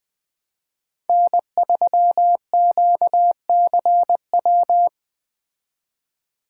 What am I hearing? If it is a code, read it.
Morse code, heard as a keyed tone, about 20 wpm, 700 Hz. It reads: N3QCW
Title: Speed Racing - USA 1x3 Call Signs 17wpm